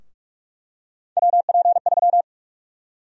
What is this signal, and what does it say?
Morse code, heard as a keyed tone, about 45 wpm, 700 Hz. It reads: WP3
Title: USA Call Sign Prefixes - 45wpm